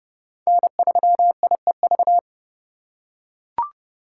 Morse code, heard as a keyed tone, about 30 wpm, 700 Hz. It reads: N3IEV
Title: USA Call Signs - 2x Character Spacing - 30wpm